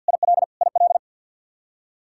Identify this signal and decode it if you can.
Morse code, heard as a keyed tone, about 49 wpm, 700 Hz. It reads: IL IL
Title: K1USN SST QSO Practice 50wpm